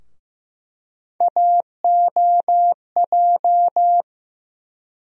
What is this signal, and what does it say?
Morse code, heard as a keyed tone, about 15 wpm, 700 Hz. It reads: AOJ
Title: USA Call Sign Postfixes - 3 Letters 15wpm